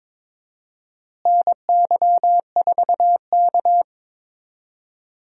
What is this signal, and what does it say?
Morse code, heard as a keyed tone, about 22 wpm, 700 Hz. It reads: NY4K
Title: USA Call Signs - 4 Characters - Repeated 3x - 22wpm